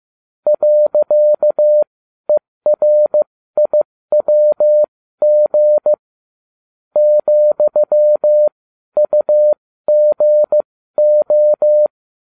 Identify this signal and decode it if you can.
Morse code, heard as a keyed tone, about 15 wpm, spaced as for 12 wpm, 600 Hz. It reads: .ERIWG ,UGO